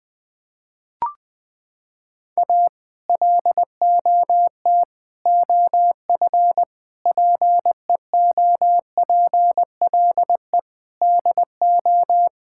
Morse code, heard as a keyed tone, about 20 wpm, 700 Hz. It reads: A LOT OF PEOPLE DO
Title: Sentences with 5-Word Phrases - 20wpm